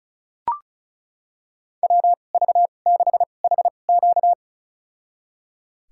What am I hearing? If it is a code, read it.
Morse code, heard as a keyed tone, about 34 wpm, spaced as for 25 wpm, 700 Hz. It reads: WV6HQ
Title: USA Call Signs - 5 Characters - 2x Character Spacing - 35wpm